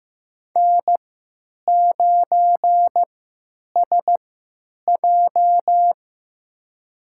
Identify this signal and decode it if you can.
Morse code, heard as a keyed tone, about 15 wpm, spaced as for 9 wpm, 700 Hz. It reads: N9SJ
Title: USA Call Signs - 3x Character Spacing - 15wpm